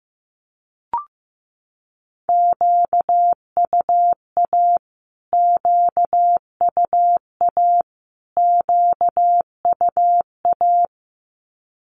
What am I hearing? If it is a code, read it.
Morse code, heard as a keyed tone, about 15 wpm, 700 Hz. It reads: QUA QUA QUA
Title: Q-Codes Repeated 3x 15wpm